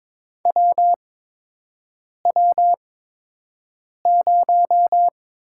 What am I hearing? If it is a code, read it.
Morse code, heard as a keyed tone, about 22 wpm, spaced as for 6 wpm, 700 Hz. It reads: WW0
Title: USA Call Signs - 8x Character Spacing - 22wpm